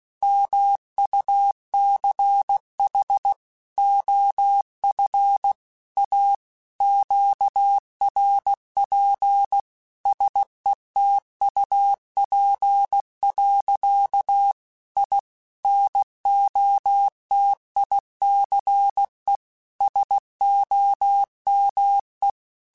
Morse code, heard as a keyed tone, about 16 wpm, 800 Hz. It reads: MUCH OF A QRP SETUP. I NOTICE SOME
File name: QSOs / QSOs-mp3s / WD0BGP 1@15